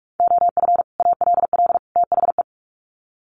Morse code, heard as a keyed tone, about 44 wpm, 700 Hz. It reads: OF ALL THE
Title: Sets of 3 Words 45wpm